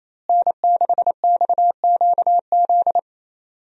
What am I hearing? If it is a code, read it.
Morse code, heard as a keyed tone, about 28 wpm, 700 Hz. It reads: N6XQZ
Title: USA Call Signs - 5 Characters - Repeated 2x - 28wpm